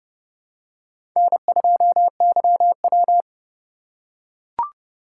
Morse code, heard as a keyed tone, about 32 wpm, 700 Hz. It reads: N2YW
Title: USA Call Signs - 2x Character Spacing - 30wpm